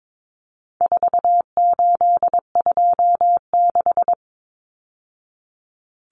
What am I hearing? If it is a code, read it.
Morse code, heard as a keyed tone, about 22 wpm, 700 Hz. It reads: 4826
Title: Numbers - 4-Digit 22wpm